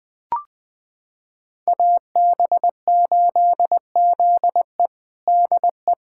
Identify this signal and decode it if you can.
Morse code, heard as a keyed tone, about 20 wpm, 700 Hz. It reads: AB8ZE DE
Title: USA Callsign DE Callsign - 5 Characters - 20wpm